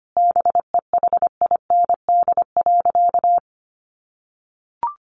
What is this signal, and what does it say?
Morse code, heard as a keyed tone, about 25 wpm, 700 Hz. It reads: BEHIND.
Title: Sentences from Top 1400 Words 25wpm